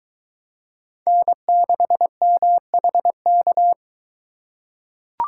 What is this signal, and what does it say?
Morse code, heard as a keyed tone, about 22 wpm, 700 Hz. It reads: N6MHK